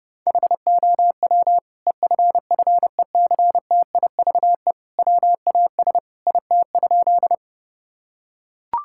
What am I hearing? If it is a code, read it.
Morse code, heard as a keyed tone, about 32 wpm, 700 Hz. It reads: HOW EFFECTIVE WAS IT?
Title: Sentences from Top 1000 Words - Review Part 6 - 30wpm